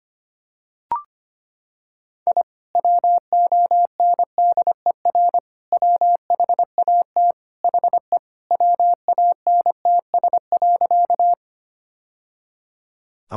Morse code, heard as a keyed tone, about 25 wpm, 700 Hz. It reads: I WONDER WHAT HE WANTS.
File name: practice / 25wpm / Sentences with 2-Word Phrases - 25wpm